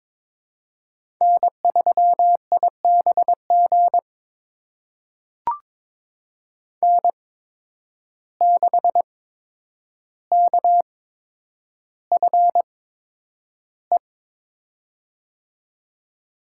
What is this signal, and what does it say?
Morse code, heard as a keyed tone, about 22 wpm, spaced as for 9 wpm, 700 Hz. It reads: N3IBG N6KFE
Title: USA Call Signs - 5 Characters - 8x Character Spacing - 22wpm